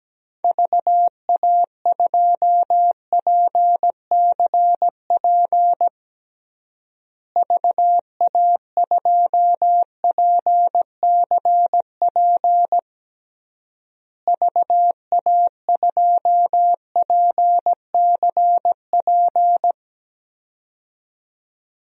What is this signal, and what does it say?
Morse code, heard as a keyed tone, about 17 wpm, 700 Hz. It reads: VA2PCP VA2PCP VA2PCP
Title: DXCC-100 Call Signs - 6 Characters - Repeated 3x - 17wpm